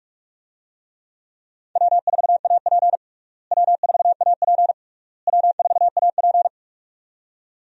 Morse code, heard as a keyed tone, about 42 wpm, 700 Hz. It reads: W4AP W4AP W4AP
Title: USA Call Signs - 4 Characters - Repeated 3x - 45wpm